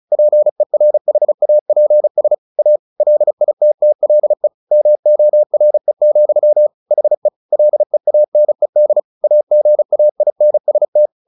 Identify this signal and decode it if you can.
Morse code, heard as a keyed tone, about 34 wpm, 600 Hz. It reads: PERHAPS A LITTLE MORE, HE LEANED AGAINST
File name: the old man and the sea / the old man and the sea 1 35wpm